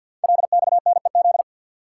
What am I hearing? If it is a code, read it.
Morse code, heard as a keyed tone, about 52 wpm, 700 Hz. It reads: LXNEZ
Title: Code Groups - 5 Letters - 50wpm